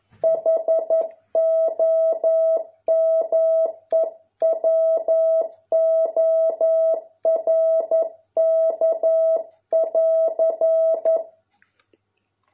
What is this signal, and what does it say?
Morse code, heard as a keyed tone, about 11 wpm, 650 Hz. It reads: HOMEWORK<AR>